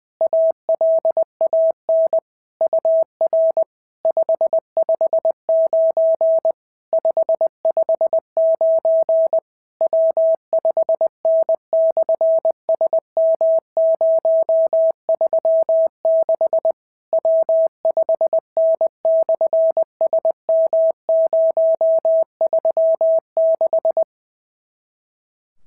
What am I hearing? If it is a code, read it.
Morse code, heard as a keyed tone, about 20 wpm, 650 Hz. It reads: ALAN UR 559 559 W5N/SM036 W5N/SM036
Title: SOTA QSO Practice 20wpm